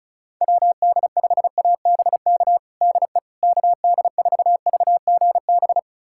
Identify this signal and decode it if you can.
Morse code, heard as a keyed tone, about 36 wpm, 700 Hz. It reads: WD5ABK DE KD4VGB
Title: USA Callsign DE Callsign 35wpm